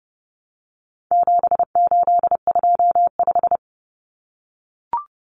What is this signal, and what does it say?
Morse code, heard as a keyed tone, about 30 wpm, 700 Hz. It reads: 7825
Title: Numbers - 4-Digit 30wpm